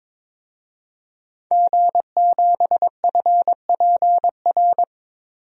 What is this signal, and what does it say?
Morse code, heard as a keyed tone, about 22 wpm, 700 Hz. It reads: G7FPR
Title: DXCC-100 Call Signs - 5 Characters - Repeated 2x - 22wpm